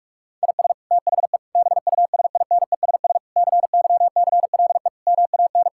Morse code, heard as a keyed tone, about 46 wpm, 700 Hz. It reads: IS THE BUSINESS CYCLE MAN
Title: Sentences from Top 1000 Words - Review Part 6 - 45wpm